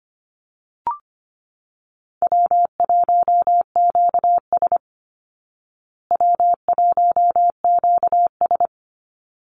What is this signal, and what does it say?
Morse code, heard as a keyed tone, about 25 wpm, 700 Hz. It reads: W1QS W1QS
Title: Call Signs Repeated 3x - 25wpm